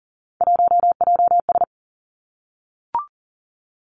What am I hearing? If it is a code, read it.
Morse code, heard as a keyed tone, about 38 wpm, 700 Hz. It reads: 1JS
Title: Three Challenging Letter-Number Combinations 40wpm